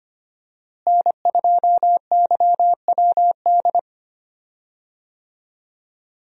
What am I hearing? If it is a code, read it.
Morse code, heard as a keyed tone, about 25 wpm, 700 Hz. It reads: N2YWD
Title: USA Call Signs - 5 Characters - Repeated 3x - 25wpm